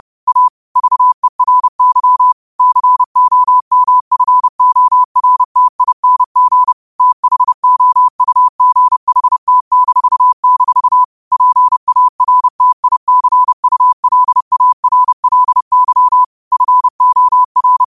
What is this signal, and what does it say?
Morse code, heard as a keyed tone, about 30 wpm, 1000 Hz. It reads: A VERY COMFORTING THOUGHT-- PARTICULARLY FOR
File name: qotdfiles / 30wpm